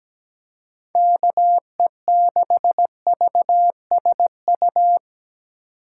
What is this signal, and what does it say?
Morse code, heard as a keyed tone, about 17 wpm, 700 Hz. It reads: KE6VSU
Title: USA Call Signs - 6 Characters - 3x Character Spacing - 17wpm